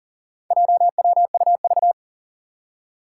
Morse code, heard as a keyed tone, about 37 wpm, 700 Hz. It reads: JWUV
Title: Four Challenging Letter-Number Combinations 40wpm